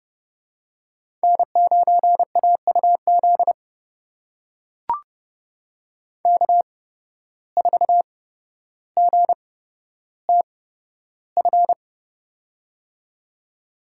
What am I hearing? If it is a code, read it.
Morse code, heard as a keyed tone, about 30 wpm, spaced as for 13 wpm, 700 Hz. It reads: N9AUZ K4GTF
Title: USA Call Signs - 5 Characters - 8x Character Spacing - 30wpm